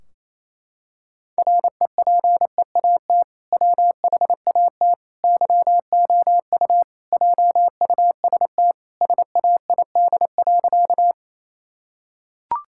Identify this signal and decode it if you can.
Morse code, heard as a keyed tone, about 28 wpm, 700 Hz. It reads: REPEAT WHAT YOU JUST SAID.
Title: Sentences from Top 1600 Words 28wpm